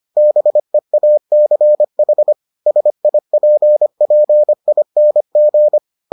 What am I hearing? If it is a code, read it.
Morse code, heard as a keyed tone, about 25 wpm, 600 Hz. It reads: BEACH SIPPING